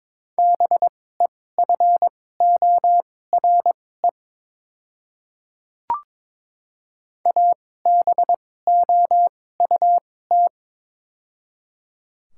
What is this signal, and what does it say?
Morse code, heard as a keyed tone, about 22 wpm, spaced as for 16 wpm, 700 Hz. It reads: BEFORE ABOUT
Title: 2x Character Spacing - Top 200 Words - 22wpm